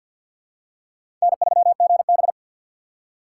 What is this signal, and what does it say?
Morse code, heard as a keyed tone, about 48 wpm, 700 Hz. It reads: N3GB
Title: USA Call Signs - 6x Character Spacing - 50wpm